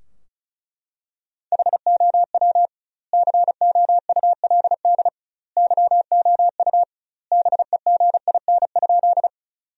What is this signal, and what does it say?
Morse code, heard as a keyed tone, about 36 wpm, 700 Hz. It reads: HOW COULD YOU BEGIN?
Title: 2x Word-Spacing - Sentences from Top 300 Words - 35wpm